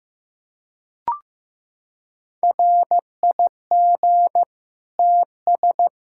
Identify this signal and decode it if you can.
Morse code, heard as a keyed tone, about 15 wpm, 700 Hz. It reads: RIG TS
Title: K1USN SST QSO Practice 15wpm